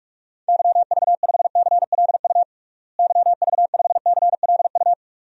Morse code, heard as a keyed tone, about 46 wpm, 700 Hz. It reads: YV5CLU YV5CLU
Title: DXCC-100 Call Signs - 6 Characters - Repeated 2x - 45wpm